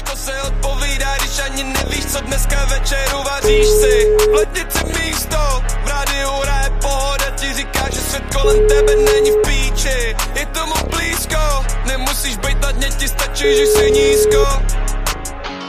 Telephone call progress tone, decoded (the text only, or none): ringback tone